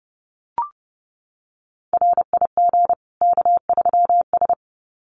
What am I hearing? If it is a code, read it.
Morse code, heard as a keyed tone, about 28 wpm, 700 Hz. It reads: RIG K3S
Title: Common QSO Phrases 30wpm